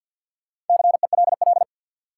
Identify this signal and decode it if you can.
Morse code, heard as a keyed tone, about 50 wpm, 700 Hz. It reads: CELL
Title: Sentences from Top 1100 Words - New Word Drill Alt - 50wpm